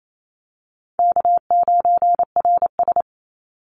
Morse code, heard as a keyed tone, about 28 wpm, 700 Hz. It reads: K9RS
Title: Call Signs - 28wpm